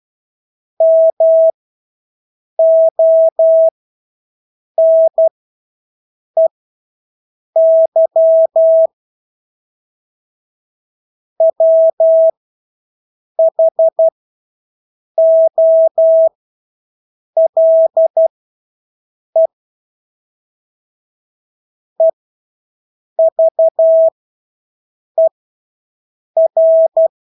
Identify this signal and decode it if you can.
Morse code, heard as a keyed tone, about 12 wpm, spaced as for 6 wpm, 650 Hz. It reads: MONEY WHOLE EVER
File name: Engelske ord 12-6 wpm